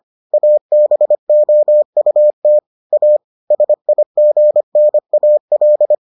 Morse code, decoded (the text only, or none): ABOUT A SIGNAL